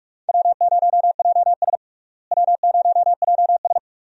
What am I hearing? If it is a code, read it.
Morse code, heard as a keyed tone, about 46 wpm, 700 Hz. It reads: W0JS W0JS
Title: USA Call Signs - 4 Characters - Repeated 3x - 45wpm